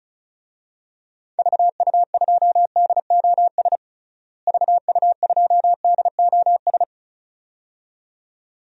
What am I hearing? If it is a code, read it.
Morse code, heard as a keyed tone, about 36 wpm, 700 Hz. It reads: VU2DOS VU2DOS
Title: DXCC-100 Call Signs - 6 Characters - Repeated 2x - 35wpm